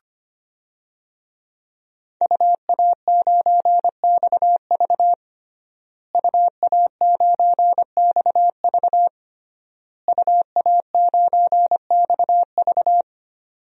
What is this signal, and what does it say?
Morse code, heard as a keyed tone, about 25 wpm, 700 Hz. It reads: UA9XV UA9XV UA9XV
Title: DXCC-100 Call Signs - 5 Characters - Repeated 3x - 25wpm